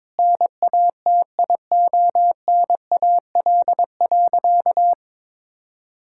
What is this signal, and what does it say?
Morse code, heard as a keyed tone, about 22 wpm, 700 Hz. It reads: NATIONAL.
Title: Sentences from Top 1000 Words - Review Part 5 - 22wpm